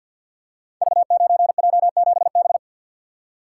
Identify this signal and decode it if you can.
Morse code, heard as a keyed tone, about 47 wpm, 700 Hz. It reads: V9J7B
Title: Code Groups - 5 Letter:Numbers - 50wpm